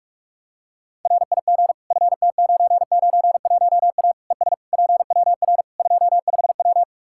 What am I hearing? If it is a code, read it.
Morse code, heard as a keyed tone, about 47 wpm, 700 Hz. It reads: RIG FT991A ES PWR 25W
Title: Practice QSO - Exchange 2 - Info - 45wpm